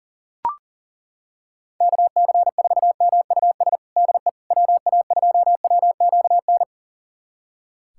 Morse code, heard as a keyed tone, about 40 wpm, 700 Hz. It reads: KC4MUS DE WA2WQN
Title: USA Callsign DE Callsign 40wpm